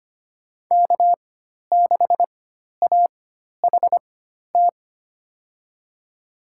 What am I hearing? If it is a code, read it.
Morse code, heard as a keyed tone, about 25 wpm, spaced as for 12 wpm, 700 Hz. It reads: K6AHT
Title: USA Call Signs - 5 Characters - 4x Character Spacing - 25wpm